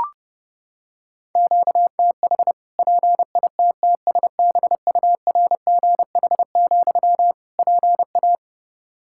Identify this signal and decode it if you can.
Morse code, heard as a keyed tone, about 30 wpm, 700 Hz. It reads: QTH PITTSBURGH, PA
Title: Practice QSO - Exchange 1 - Beginning - 30wpm